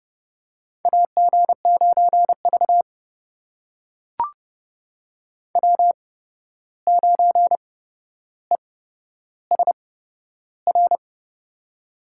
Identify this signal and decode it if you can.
Morse code, heard as a keyed tone, about 30 wpm, spaced as for 12 wpm, 700 Hz. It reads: AG9V W9ESR
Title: USA Call Signs - 8x Character Spacing - 30wpm